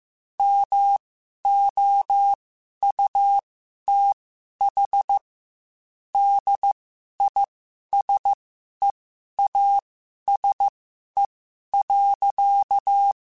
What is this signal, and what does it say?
Morse code, heard as a keyed tone, about 15 wpm, spaced as for 11 wpm, 800 Hz. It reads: MOUTH DISEASE.
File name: Quotes / Quotes-mp3s / HoofInMouth @13